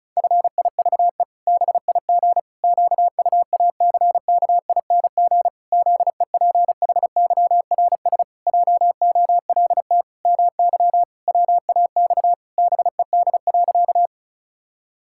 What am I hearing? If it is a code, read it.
Morse code, heard as a keyed tone, about 34 wpm, 700 Hz. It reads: FIVE BIG QUACKING ZEPHYRS JOLT MY WAX BED.
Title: Pangram Challenge v2 - 35wpm